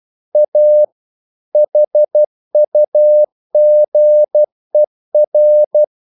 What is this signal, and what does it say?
Morse code, heard as a keyed tone, about 12 wpm, 600 Hz. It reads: A HUGER
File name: mp3 files / the gardener - 12wpm